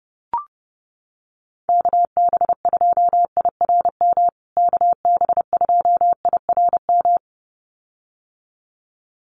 Morse code, heard as a keyed tone, about 30 wpm, 700 Hz. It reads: KB2IRM KB2IRM
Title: USA Call Signs Repeated 2x 30wpm